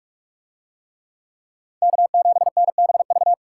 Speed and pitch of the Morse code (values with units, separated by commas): 42 wpm, 700 Hz